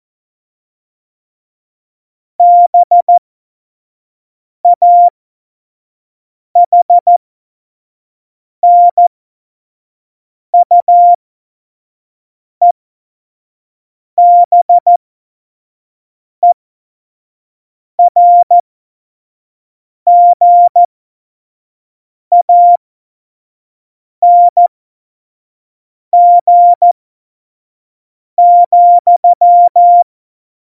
Morse code, heard as a keyed tone, about 14 wpm, spaced as for 5 wpm, 700 Hz. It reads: BAHNUEBERGANG,
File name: Morsekurs / Witze / Witz 19. Februar 2019, 25 F70